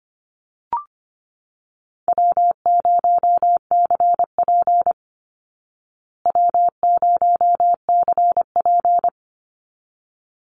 Morse code, heard as a keyed tone, about 25 wpm, 700 Hz. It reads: W0CP W0CP
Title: Easy Call Signs Repeated 2x - 25wpm